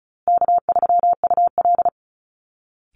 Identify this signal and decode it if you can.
Morse code, heard as a keyed tone, about 34 wpm, 700 Hz. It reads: K3UL
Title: Easy Call Signs Repeated 3x - 35wpm